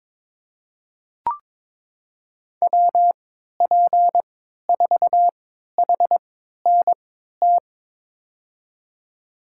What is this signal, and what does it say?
Morse code, heard as a keyed tone, about 23 wpm, spaced as for 13 wpm, 700 Hz. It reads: WP4HNT
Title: USA Call Signs - 6 Characters - 3x Character Spacing - 22wpm